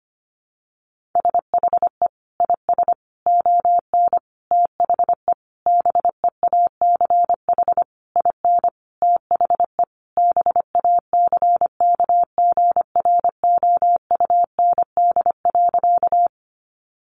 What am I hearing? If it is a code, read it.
Morse code, heard as a keyed tone, about 25 wpm, 700 Hz. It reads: SHE IS ON THE BEACH IN THE BACKGROUND.
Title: Sentences from Top 1500 Words 25wpm